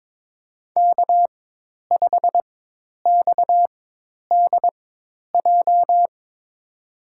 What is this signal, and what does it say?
Morse code, heard as a keyed tone, about 22 wpm, spaced as for 10 wpm, 700 Hz. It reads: K5XDJ
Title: USA Call Signs - 5 Characters - 4x Character Spacing - 22wpm